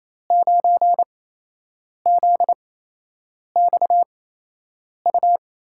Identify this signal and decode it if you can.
Morse code, heard as a keyed tone, about 27 wpm, spaced as for 8 wpm, 700 Hz. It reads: 9ZXU
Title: USA Call Signs - 5 Characters - 8x Character Spacing - 28wpm